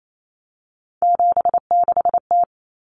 Morse code, heard as a keyed tone, about 28 wpm, 700 Hz. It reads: 76T
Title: Common Cut Numbers - 3-Digit 28wpm